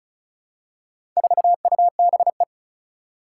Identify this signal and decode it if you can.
Morse code, heard as a keyed tone, about 36 wpm, 700 Hz. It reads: 4UBE
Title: Four Challenging Letter-Number Combinations 35wpm